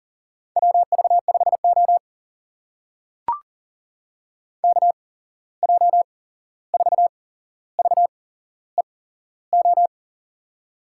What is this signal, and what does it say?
Morse code, heard as a keyed tone, about 40 wpm, spaced as for 15 wpm, 700 Hz. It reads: WV5O KJ4VEO
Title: USA Call Signs - 8x Character Spacing - 40wpm